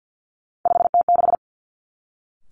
Morse code, heard as a keyed tone, about 50 wpm, 700 Hz. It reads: 5T6
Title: Three Challenging Letter-Number Combinations 50wpm